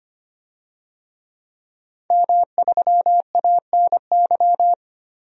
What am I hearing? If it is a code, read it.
Morse code, heard as a keyed tone, about 25 wpm, 700 Hz. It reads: M3ANY
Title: DXCC-100 Call Signs - 5 Characters - Repeated 3x - 25wpm